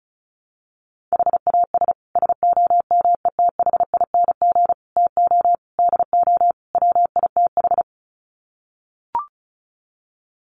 Morse code, heard as a keyed tone, about 34 wpm, 700 Hz. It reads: HAS SOMETHING TO DO WITH
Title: Sets of 5 Words Encore 35wpm